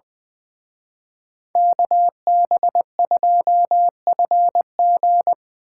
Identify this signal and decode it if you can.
Morse code, heard as a keyed tone, about 24 wpm, 700 Hz. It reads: KB2FG
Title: Speed Racing - USA 2x2 Call Signs 20wpm